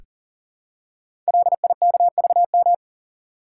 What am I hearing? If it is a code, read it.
Morse code, heard as a keyed tone, about 40 wpm, 700 Hz. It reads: LIKVM
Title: Code Groups - 5 Letters - 40wpm